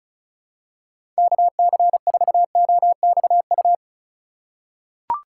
Morse code, heard as a keyed tone, about 34 wpm, 700 Hz. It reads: KC4OXU
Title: USA Call Signs - 6 Characters - 8x Character Spacing - 35wpm